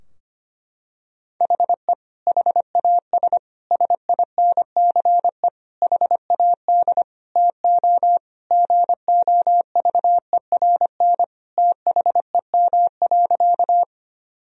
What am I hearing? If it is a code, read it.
Morse code, heard as a keyed tone, about 25 wpm, 700 Hz. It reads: HE HAS SINCE HAD TO GOVERN THEM.
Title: Sentences from Top 1000 Words - Review Part 2 - 25wpm